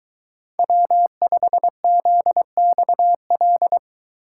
Morse code, heard as a keyed tone, about 22 wpm, 700 Hz. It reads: W5ZXL